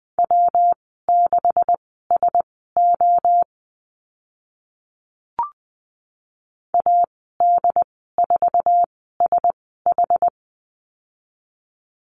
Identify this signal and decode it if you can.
Morse code, heard as a keyed tone, about 20 wpm, spaced as for 15 wpm, 700 Hz. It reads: W6SO AD4SH
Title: 161 Lesson 26g - Letter g - Call Signs 20wpm